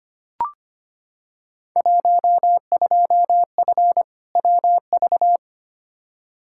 Morse code, heard as a keyed tone, about 25 wpm, 700 Hz. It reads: 12F WV
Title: Field Day Exchanges Encore - 25wpm